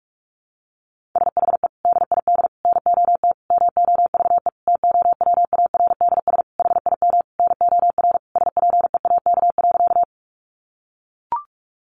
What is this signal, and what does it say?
Morse code, heard as a keyed tone, about 44 wpm, 700 Hz. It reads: SHE DID NOT MOVE TOWARDS HIM NOR SPEAK.
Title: Sentences from Top 1500 Words 45wpm